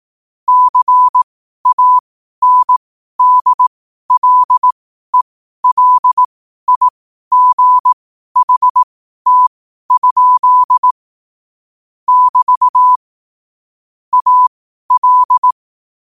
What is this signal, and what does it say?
Morse code, heard as a keyed tone, about 18 wpm, spaced as for 12 wpm, 1000 Hz. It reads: CANDLELIGHT? <BT> AL
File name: qotdfiles / 12wpm